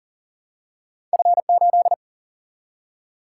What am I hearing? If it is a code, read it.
Morse code, heard as a keyed tone, about 40 wpm, 700 Hz. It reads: F 8
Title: Single-Letter Single-Number 40wpm